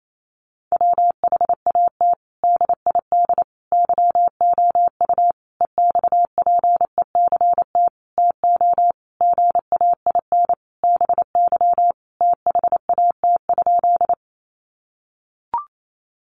Morse code, heard as a keyed tone, about 28 wpm, 700 Hz. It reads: WHAT DID YOU EXPECT TO GAIN BY THAT?
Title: Sentences from Top 1100 Words 28wpm